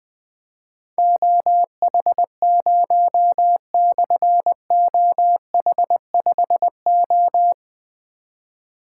Morse code, heard as a keyed tone, about 20 wpm, 700 Hz. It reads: OH0/OH5O
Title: Difficult Call Signs 20wpm